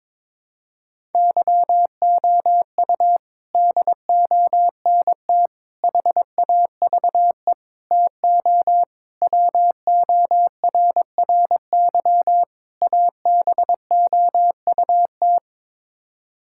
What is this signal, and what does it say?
Morse code, heard as a keyed tone, about 21 wpm, 700 Hz. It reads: YOU DONT HAVE TO WORRY ABOUT
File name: practice / 22wpm / Sets of 6 Words - 22wpm